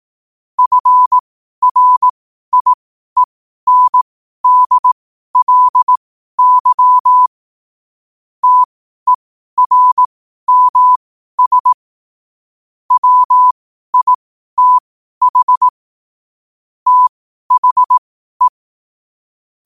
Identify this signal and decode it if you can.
Morse code, heard as a keyed tone, about 18 wpm, spaced as for 12 wpm, 1000 Hz. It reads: FRIENDLY TERMS WITH THE
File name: qotdfiles / 12wpm